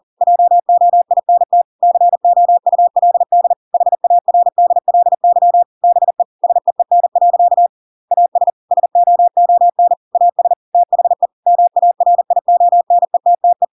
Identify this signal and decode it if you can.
Morse code, heard as a keyed tone, about 40 wpm, 700 Hz. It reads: JOINT COULD HARDLY BE SEEN. AS SOON AS THE MARIONETTE
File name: Pinocchio / Pinocchio ch8 40 40 ik1hge